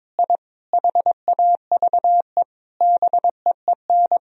Morse code, heard as a keyed tone, about 22 wpm, 700 Hz. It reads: I HAVE BEEN
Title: Sentences from Top 1000 Words - Review Part 3 - 22wpm